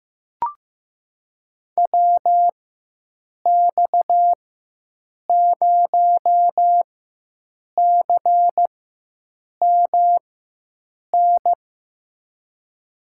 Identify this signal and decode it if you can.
Morse code, heard as a keyed tone, about 15 wpm, spaced as for 7 wpm, 700 Hz. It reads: WX0CMN
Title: USA Call Signs - 4x Character Spacing - 15wpm